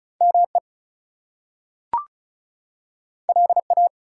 Morse code, heard as a keyed tone, about 36 wpm, 700 Hz. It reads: ME LA
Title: US States and Canadian Provinces 35wpm